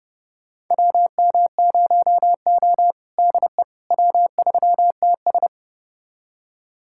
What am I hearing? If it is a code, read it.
Morse code, heard as a keyed tone, about 30 wpm, 700 Hz. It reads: WM0O DE W3TS
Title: USA Callsign DE Callsign - 4 Characters - 30wpm